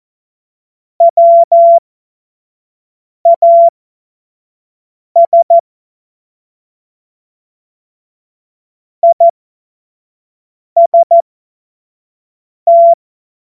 Morse code, heard as a keyed tone, about 14 wpm, spaced as for 5 wpm, 680 Hz. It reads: WAS IST
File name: Morsekurs / Witze / Witz 27. Januar 2026 25 F70